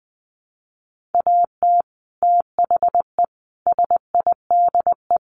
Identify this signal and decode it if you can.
Morse code, heard as a keyed tone, about 20 wpm, 700 Hz. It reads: AT THE SIDE
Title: Sets of 5 Words Encore2 20wpm